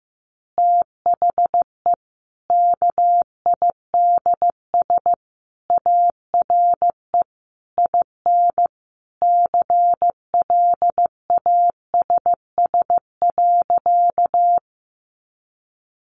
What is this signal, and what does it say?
Morse code, heard as a keyed tone, about 15 wpm, 700 Hz. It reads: THE KIDS ARE IN CLASS.
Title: Easy Phrases 15wpm